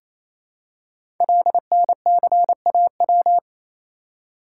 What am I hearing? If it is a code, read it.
Morse code, heard as a keyed tone, about 28 wpm, 700 Hz. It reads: LNCAW
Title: Code Groups - 5 Letters - 28wpm